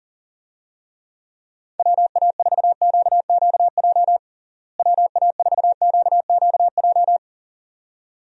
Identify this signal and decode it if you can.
Morse code, heard as a keyed tone, about 37 wpm, 700 Hz. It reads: WA4QQJ WA4QQJ
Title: USA Call Signs - 6 Characters - Repeated 2x - 40wpm